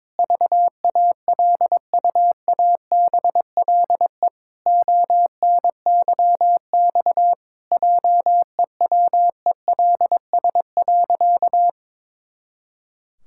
Morse code, heard as a keyed tone, about 22 wpm, 700 Hz. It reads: VALUABLE ONYX JEWELS.
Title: Pangram Challenge v2 - 22wpm